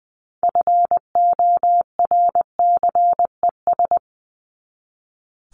Morse code, heard as a keyed tone, about 19 wpm, 700 Hz. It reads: FORCES
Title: ES:S Suffix - Top 100 Words with Base Words 20wpm